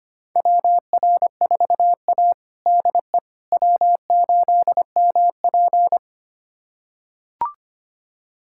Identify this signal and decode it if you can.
Morse code, heard as a keyed tone, about 25 wpm, 700 Hz. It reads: WR4A DE W8MP
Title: USA Callsign DE Callsign - 4 Characters - 25wpm